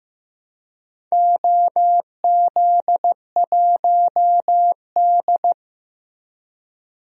Morse code, heard as a keyed tone, about 15 wpm, 700 Hz. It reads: OZ1D
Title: DXCC-100 Call Signs - 4 Characters - Repeated 2x - 15wpm